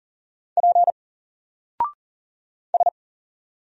Morse code, heard as a keyed tone, about 38 wpm, 700 Hz. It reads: P S
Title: Single Character (Letter:Number:Punctuation:Prosign) Alt 40wpm